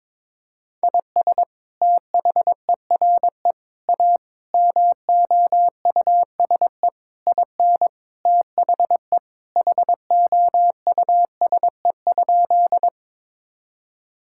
Morse code, heard as a keyed tone, about 22 wpm, 700 Hz. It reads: IS THERE A MOUSE IN THE HOUSE?
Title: Sentences from Top 1000 Words - Review Part 5 - 22wpm